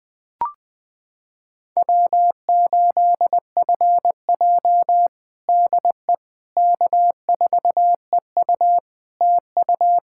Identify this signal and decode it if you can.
Morse code, heard as a keyed tone, about 21 wpm, 700 Hz. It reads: W8FJ DE K4EU TU
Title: Practice QSO - Exchange 4 - Ending - 20wpm